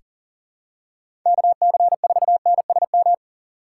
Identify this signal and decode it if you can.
Morse code, heard as a keyed tone, about 38 wpm, 700 Hz. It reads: KC4NSM
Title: USA Call Signs - 6 Characters - 6x Character Spacing - 40wpm